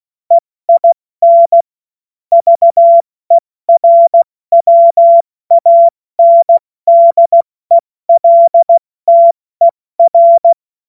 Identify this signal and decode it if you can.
Morse code, heard as a keyed tone, about 16 wpm, 680 Hz. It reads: EIN VERWANDELTER
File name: Morsekurs / Witze / Witz 08. November 2022 70 F80